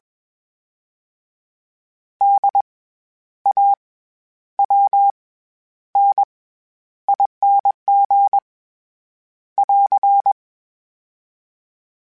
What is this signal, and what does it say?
Morse code, heard as a keyed tone, about 20 wpm, 800 Hz. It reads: D A W N ING <AR>
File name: Special INGs-mp3s / Special INGs-mp3s / SlowToFaster ing @20